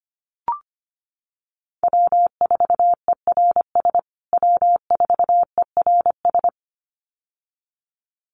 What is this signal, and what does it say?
Morse code, heard as a keyed tone, about 25 wpm, 700 Hz. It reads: W4ERS W4ERS
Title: USA Call Signs Repeated 2x 25wpm